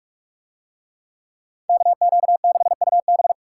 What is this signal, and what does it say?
Morse code, heard as a keyed tone, about 44 wpm, 700 Hz. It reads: KQ6UB